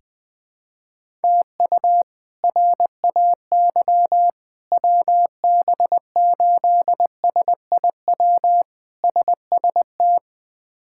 Morse code, heard as a keyed tone, about 20 wpm, 700 Hz. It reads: TU RAY WB8SIW SST
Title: K1USN SST QSO Practice - 20wpm